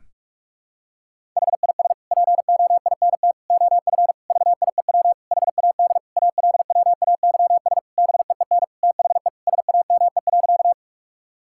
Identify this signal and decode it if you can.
Morse code, heard as a keyed tone, about 46 wpm, 700 Hz. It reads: HIS POINT OF VIEW HAD ALWAYS BEEN THE SAME.
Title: Sentences from Top 1000 Words - Review Part 6 - 45wpm